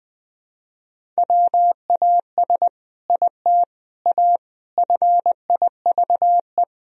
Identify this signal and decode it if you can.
Morse code, heard as a keyed tone, about 20 wpm, 700 Hz. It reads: WAS IT A FIVE
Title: Sentences from Top 1000 Words - Review Part 2 - 20wpm